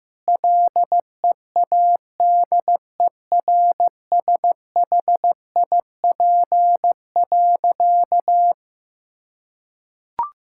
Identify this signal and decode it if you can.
Morse code, heard as a keyed tone, about 15 wpm, 700 Hz. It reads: LEADERSHIP.
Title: Sentences from Top 1600 Words 15wpm